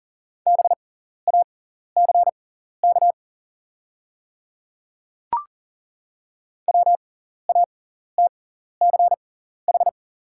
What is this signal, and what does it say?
Morse code, heard as a keyed tone, about 40 wpm, spaced as for 14 wpm, 700 Hz. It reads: BACK WATCH
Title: 6x Character Spacing - Top 500 Words - 40wpm